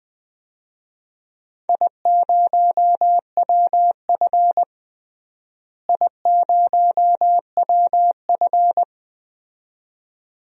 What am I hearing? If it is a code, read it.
Morse code, heard as a keyed tone, about 20 wpm, 700 Hz. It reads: I0WF I0WF
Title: DXCC-100 Call Signs - 4 Characters - Repeated 2x - 20wpm